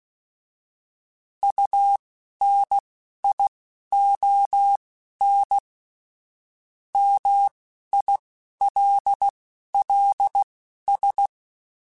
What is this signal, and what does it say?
Morse code, heard as a keyed tone, about 16 wpm, spaced as for 11 wpm, 800 Hz. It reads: UNION MILLS